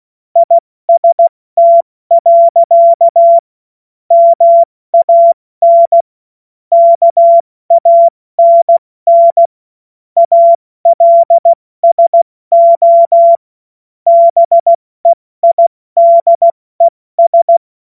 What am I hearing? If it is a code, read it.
Morse code, heard as a keyed tone, about 16 wpm, spaced as for 14 wpm, 680 Hz. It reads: IST. MAN KANN ALSO BEIDES
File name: Morsekurs / Witze / Witz 21. Juni 2022 70 F80